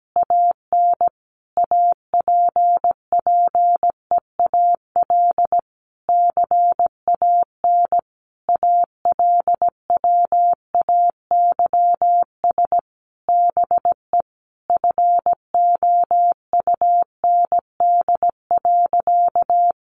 Morse code, heard as a keyed tone, about 17 wpm, 700 Hz. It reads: AN APPEAL CAN ALWAYS BE FOUND.
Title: Sentences from Top 1300 Words 17wpm